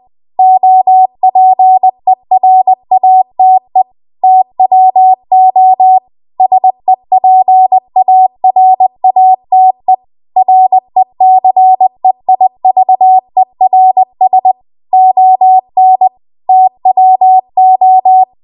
Morse code, heard as a keyed tone, about 20 wpm, 750 Hz. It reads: OPERATE TWO SEPARATE RECEIVERS ON TWO
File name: Archive / 20 WPM / 20WPM